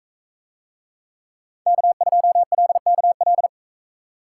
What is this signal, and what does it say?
Morse code, heard as a keyed tone, about 41 wpm, 700 Hz. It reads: K2LKL